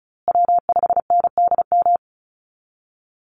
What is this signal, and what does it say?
Morse code, heard as a keyed tone, about 34 wpm, 700 Hz. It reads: W5NDM
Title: USA Call Signs Repeated 2x 35wpm